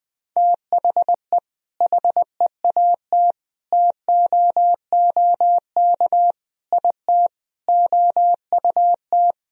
Morse code, heard as a keyed tone, about 20 wpm, 700 Hz. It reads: THE HEAT TOOK IT OUT
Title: Sentences from Top 1000 Words - Review Part 3 - 20wpm